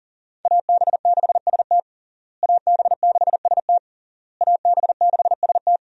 Morse code, heard as a keyed tone, about 40 wpm, 700 Hz. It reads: AB6ST AB6ST AB6ST
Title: USA Call Signs - 5 Characters - Repeated 3x - 40wpm